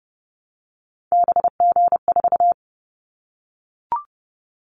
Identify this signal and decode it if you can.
Morse code, heard as a keyed tone, about 30 wpm, 700 Hz. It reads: BG4
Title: Callsign Prefixes for Top 100 DXCC - 30wpm